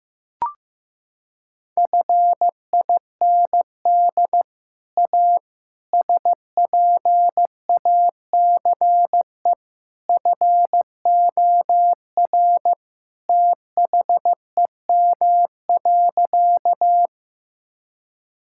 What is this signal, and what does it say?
Morse code, heard as a keyed tone, about 16 wpm, 700 Hz. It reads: FIND A SPACE FOR THEM.
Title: Sentences from Top 1000 Words - Review Part 5 - 15wpm